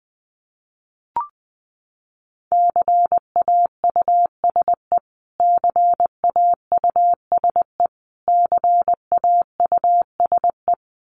Morse code, heard as a keyed tone, about 20 wpm, 700 Hz. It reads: CAUSE CAUSE CAUSE
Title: Top 500 Words Repeated 20wpm